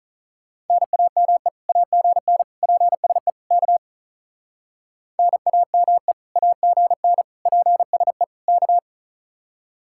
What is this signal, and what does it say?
Morse code, heard as a keyed tone, about 43 wpm, 700 Hz. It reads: NAME AGN PSE K NAME AGN PSE K
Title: Speed-Racing Common QSO Phrases - 35wpm